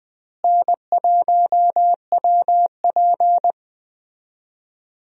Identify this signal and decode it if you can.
Morse code, heard as a keyed tone, about 20 wpm, 700 Hz. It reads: N1WP
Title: USA Call Signs - 4 Characters - Repeated 3x - 20wpm